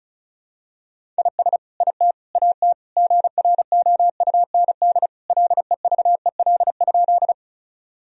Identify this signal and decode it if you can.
Morse code, heard as a keyed tone, about 34 wpm, 700 Hz. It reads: IS IT AT GROUND LEVEL?
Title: Sentences from Top 1000 Words - Review Part 4 - 35wpm